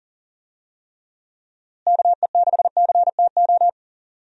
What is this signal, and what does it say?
Morse code, heard as a keyed tone, about 39 wpm, 700 Hz. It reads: KE6CTO
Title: USA Call Signs - 6 Characters - Repeated 2x - 40wpm